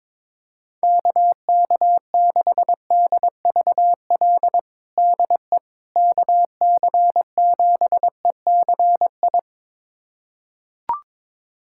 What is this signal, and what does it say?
Morse code, heard as a keyed tone, about 22 wpm, 700 Hz. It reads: KK6DVL DE KC7ECI
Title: USA Callsign DE Callsign - 6 Characters - 22wpm